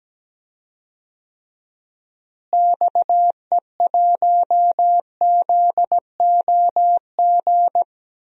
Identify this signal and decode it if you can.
Morse code, heard as a keyed tone, about 17 wpm, 700 Hz. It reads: XE1ZOG
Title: DXCC-100 Call Signs - 6 Characters - Repeated 3x - 17wpm